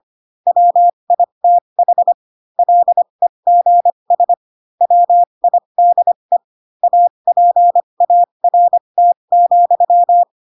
Morse code, heard as a keyed tone, about 25 wpm, spaced as for 22 wpm, 700 Hz. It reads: WITH LEGS WIDE APART,
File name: Pinocchio / Pinocchio ch3 18 25 ik1hge